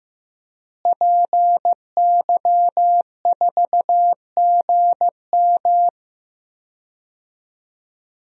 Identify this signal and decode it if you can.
Morse code, heard as a keyed tone, about 15 wpm, 700 Hz. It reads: PY4GM
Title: DXCC-100 Call Signs - 5 Characters - Repeated 3x - 15wpm